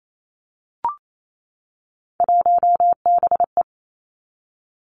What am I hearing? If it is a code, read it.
Morse code, heard as a keyed tone, about 28 wpm, 700 Hz. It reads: 1BE
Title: Three Challenging Letter-Number Combinations 28wpm